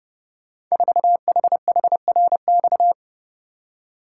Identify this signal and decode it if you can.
Morse code, heard as a keyed tone, about 32 wpm, 700 Hz. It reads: 4HHRX
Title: Code Groups - 5 Letter:Numbers - 30wpm